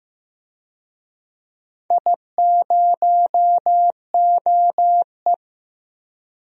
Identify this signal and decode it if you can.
Morse code, heard as a keyed tone, about 15 wpm, 700 Hz. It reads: I0OE
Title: DXCC-100 Call Signs - 4 Characters - 15wpm